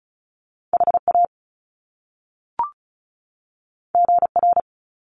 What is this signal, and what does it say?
Morse code, heard as a keyed tone, about 36 wpm, 700 Hz. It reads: HA GR
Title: Top Two Letter Combinations in Words 35wpm